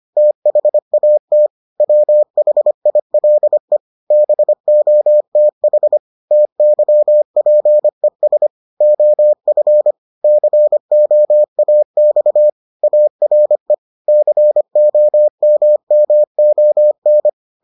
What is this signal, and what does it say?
Morse code, heard as a keyed tone, about 25 wpm, 600 Hz. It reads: THAT WHILE BOTH TYPES OF COAX ARE COMMON